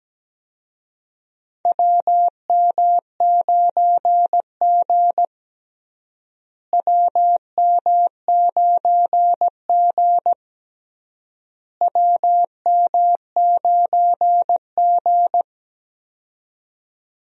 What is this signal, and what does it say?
Morse code, heard as a keyed tone, about 17 wpm, 700 Hz. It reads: WM9G WM9G WM9G
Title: USA Call Signs - 4 Characters - Repeated 3x - 17wpm